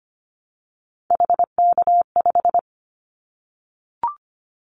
Morse code, heard as a keyed tone, about 25 wpm, 700 Hz. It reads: HK5
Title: Callsign Prefixes for Top 100 DXCC - 25wpm